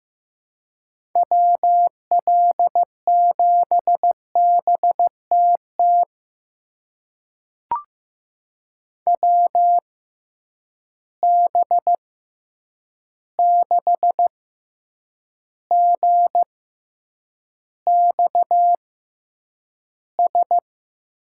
Morse code, heard as a keyed tone, about 15 wpm, spaced as for 8 wpm, 700 Hz. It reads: WL7BTT WB6GXS
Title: USA Call Signs - 6 Characters - 6x Character Spacing - 15wpm